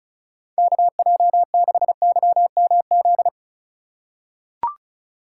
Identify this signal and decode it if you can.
Morse code, heard as a keyed tone, about 34 wpm, 700 Hz. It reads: KJ6YMZ
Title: USA Call Signs - 3x Character Spacing - 35wpm